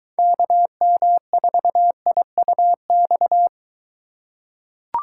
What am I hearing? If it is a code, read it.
Morse code, heard as a keyed tone, about 23 wpm, 700 Hz. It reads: KM4IUX